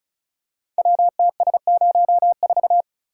Morse code, heard as a keyed tone, about 36 wpm, 700 Hz. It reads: WTS04
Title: Code Groups - 5 Letter:Numbers - 35wpm